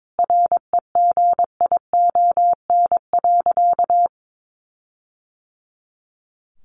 Morse code, heard as a keyed tone, about 22 wpm, 700 Hz. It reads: REGION.
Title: Sentences from Top 1500 Words 22wpm